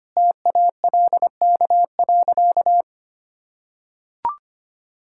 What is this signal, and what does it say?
Morse code, heard as a keyed tone, about 25 wpm, 700 Hz. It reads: TALK.
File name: practice / 25wpm / Sentences with 4-Word Phrases - 25wpm